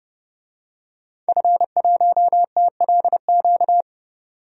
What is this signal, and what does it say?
Morse code, heard as a keyed tone, about 30 wpm, 700 Hz. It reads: F1TLQ
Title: DXCC-100 Call Signs - 5 Characters - 30wpm